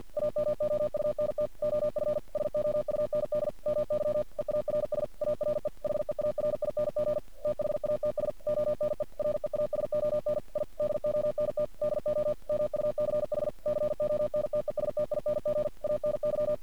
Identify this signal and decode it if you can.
Morse code, heard as a keyed tone, about 50 wpm, 600 Hz. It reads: AMOUNT OF SOUND MY EARS ARE HEARING THATS ONE REASON I DONT DO MUCH CONTESTING ANY
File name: archived / CRSnet